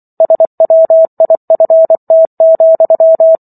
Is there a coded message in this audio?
Yes: Morse code from 0.2 to 3.4 s, heard as a keyed tone, about 24 wpm, 650 Hz. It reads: SWIFT,